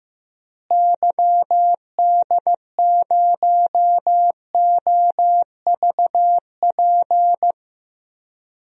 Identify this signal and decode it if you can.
Morse code, heard as a keyed tone, about 15 wpm, 700 Hz. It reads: YD0OVP
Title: DXCC-100 Call Signs - 6 Characters - Repeated 3x - 15wpm